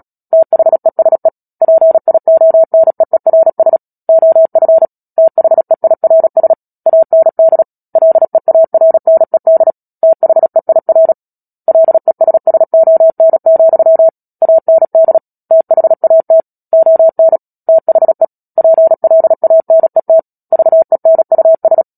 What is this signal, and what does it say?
Morse code, heard as a keyed tone, about 37 wpm, 650 Hz. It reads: THESE PIONEERS OF THEIRS AND LEARNED THEIR LESSON, AND THAT ON THE PLANET VENUS
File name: Book Two / 10-Chapter 10 - Epilogue - 36 WPM